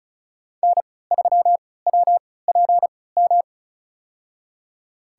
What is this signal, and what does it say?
Morse code, heard as a keyed tone, about 34 wpm, spaced as for 20 wpm, 700 Hz. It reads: N3WPM
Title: USA Call Signs - 5 Characters - 3x Character Spacing - 35wpm